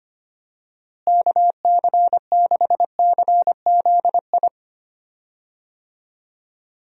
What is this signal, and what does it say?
Morse code, heard as a keyed tone, about 25 wpm, 700 Hz. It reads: KC6CZI
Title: USA Call Signs - 6 Characters - Repeated 2x - 25wpm